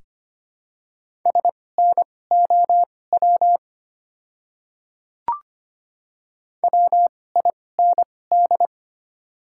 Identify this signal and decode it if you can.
Morse code, heard as a keyed tone, about 25 wpm, spaced as for 18 wpm, 700 Hz. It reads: SNOW WIND
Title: 2x Character Spacing - Top 500 Words - 25wpm